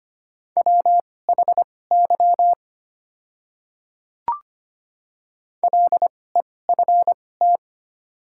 Morse code, heard as a keyed tone, about 25 wpm, spaced as for 18 wpm, 700 Hz. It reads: WHY LEFT
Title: 2x Character Spacing - Top 300 Words - 25wpm